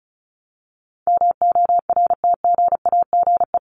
Morse code, heard as a keyed tone, about 34 wpm, 700 Hz. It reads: MORTGAGE
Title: Top 1751-2000 Words 35wpm